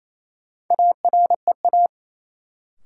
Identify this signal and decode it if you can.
Morse code, heard as a keyed tone, about 28 wpm, 700 Hz. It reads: AREA
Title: Sentences from Top 1000 Words - Review Part 2 - New Word Drill - 28wpm